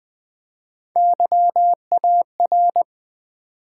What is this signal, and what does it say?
Morse code, heard as a keyed tone, about 20 wpm, 700 Hz. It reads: YAR
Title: Build A Word - 1 Letter At A Time - 3 to 5-Letter Words - 20wpm